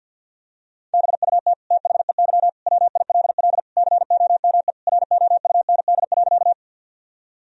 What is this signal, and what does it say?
Morse code, heard as a keyed tone, about 48 wpm, 700 Hz. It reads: BUT THEY WILL COME ROUND.
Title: Sentences from Top 1000 Words - Review Part 1 - 50wpm